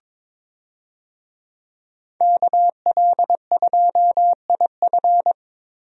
Morse code, heard as a keyed tone, about 22 wpm, 700 Hz. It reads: KL2IF
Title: USA Call Signs - 5 Characters - 22wpm